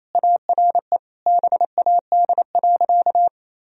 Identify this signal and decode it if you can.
Morse code, heard as a keyed tone, about 28 wpm, 700 Hz. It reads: ARE BAD.
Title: Sentences from Top 1000 Words - Review Part 2 - 28wpm